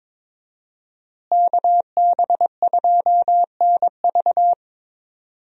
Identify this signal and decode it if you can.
Morse code, heard as a keyed tone, about 22 wpm, 700 Hz. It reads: KB2NV
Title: USA Call Signs - 5 Characters - Repeated 2x - 22wpm